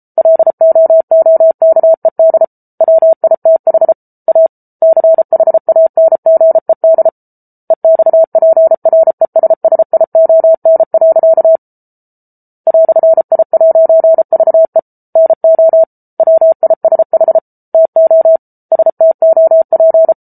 Morse code, heard as a keyed tone, about 34 wpm, 650 Hz. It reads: LOOKED WITH A CHANGED EXPRESSION. "I'VE NO WISH TO STOP